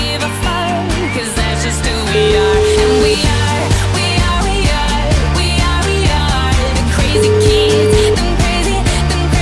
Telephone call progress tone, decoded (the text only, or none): ringback tone